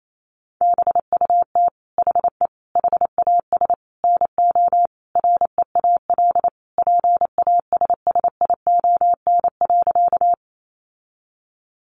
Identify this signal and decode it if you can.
Morse code, heard as a keyed tone, about 28 wpm, 700 Hz. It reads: BUT HE HAS NO REAL PASSION.
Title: Sentences from Top 1300 Words 28wpm